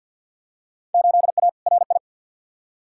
Morse code, heard as a keyed tone, about 50 wpm, 700 Hz. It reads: 8A RI
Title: Field Day Exchanges Encore - 50wpm